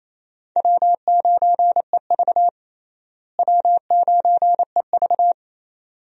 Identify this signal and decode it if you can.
Morse code, heard as a keyed tone, about 28 wpm, 700 Hz. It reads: W9EV W9EV
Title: USA Call Signs - 4 Characters - Repeated 2x - 28wpm